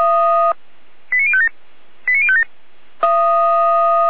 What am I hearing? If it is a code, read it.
Five-tone selective call (ZVEI), heard as quick runs of five tones, tones of about 70 ms. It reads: 89057 89057
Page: ZVEI